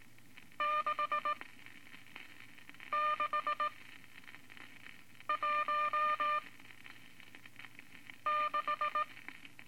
Morse code, heard as a keyed tone, about 19 wpm, 1200 Hz. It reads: 6 6 1 6
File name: Morse Code Records / 14B LCM 157 Units V Groups of Five